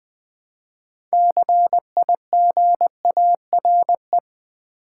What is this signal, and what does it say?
Morse code, heard as a keyed tone, about 20 wpm, 700 Hz. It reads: CIGARE
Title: Build A Word - 3 Letters At A Time - 9-Letter Words - 20wpm